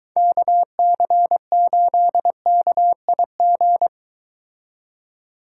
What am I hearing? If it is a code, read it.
Morse code, heard as a keyed tone, about 23 wpm, 700 Hz. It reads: KC8KIG